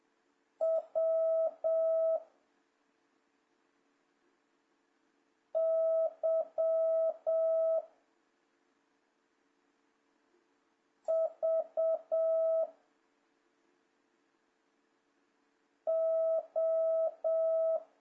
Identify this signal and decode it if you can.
Morse code, heard as a keyed tone, about 7 wpm, 660 Hz. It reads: W Y V O
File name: jh8cbh / 2026cw